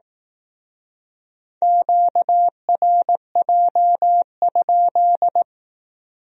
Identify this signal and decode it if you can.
Morse code, heard as a keyed tone, about 19 wpm, 700 Hz. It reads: QRJ?
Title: Speed-Racing Q-Codes - 15wpm